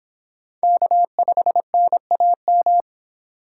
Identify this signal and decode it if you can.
Morse code, heard as a keyed tone, about 23 wpm, 700 Hz. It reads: K5NAM